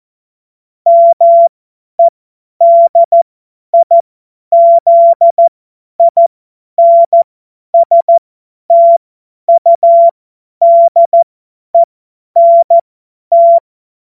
Morse code, heard as a keyed tone, about 14 wpm, spaced as for 10 wpm, 680 Hz. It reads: MEDIZINSTUDENT
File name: Morsekurs / Witze / Witz 14. Juni 2022 50 F70